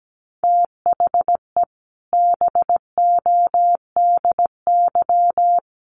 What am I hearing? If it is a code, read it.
Morse code, heard as a keyed tone, about 17 wpm, 700 Hz. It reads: THE BODY
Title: A Sound Of Thunder 17wpm